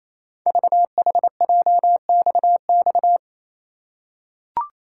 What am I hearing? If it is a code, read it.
Morse code, heard as a keyed tone, about 28 wpm, 700 Hz. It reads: VHJXX
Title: Code Groups - 5 Letters - 28wpm